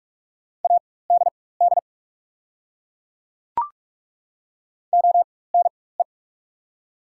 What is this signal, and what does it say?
Morse code, heard as a keyed tone, about 45 wpm, spaced as for 21 wpm, 700 Hz. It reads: ADD ONE
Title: 4x Character Spacing - Top 500 Words - 45wpm